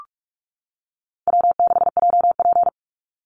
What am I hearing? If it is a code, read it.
Morse code, heard as a keyed tone, about 44 wpm, 700 Hz. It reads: W6JP
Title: NA SOTA Chaser Callsigns 45wpm